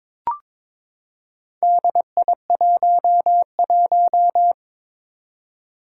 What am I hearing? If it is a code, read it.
Morse code, heard as a keyed tone, about 22 wpm, 700 Hz. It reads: DI11
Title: Four Challenging Letter-Number Combinations 22wpm